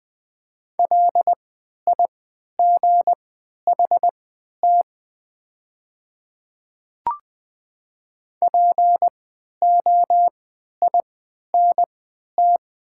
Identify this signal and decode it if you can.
Morse code, heard as a keyed tone, about 20 wpm, spaced as for 11 wpm, 700 Hz. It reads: LIGHT POINT
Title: 3x Character Spacing - Top 300 Words - 20wpm